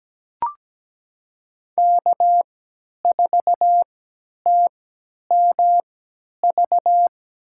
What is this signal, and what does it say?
Morse code, heard as a keyed tone, about 17 wpm, spaced as for 10 wpm, 700 Hz. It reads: K4TMV
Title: USA Call Signs - 5 Characters - 3x Character Spacing - 17wpm